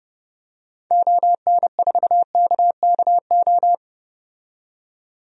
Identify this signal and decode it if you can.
Morse code, heard as a keyed tone, about 30 wpm, 700 Hz. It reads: ON4KKO
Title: DXCC-100 Call Signs - 6 Characters - 30wpm